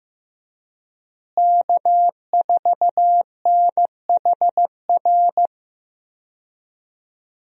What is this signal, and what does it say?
Morse code, heard as a keyed tone, about 15 wpm, 700 Hz. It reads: K4NHR
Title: USA Call Signs - 5 Characters - Repeated 3x - 15wpm